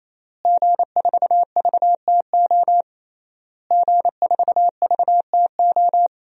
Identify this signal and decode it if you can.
Morse code, heard as a keyed tone, about 27 wpm, 700 Hz. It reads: G4VTO G4VTO
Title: DXCC-100 Call Signs - 5 Characters - Repeated 3x - 28wpm